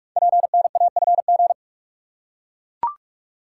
Morse code, heard as a keyed tone, about 44 wpm, 700 Hz. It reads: PNAFG
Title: Code Groups - 5 Letters - 45wpm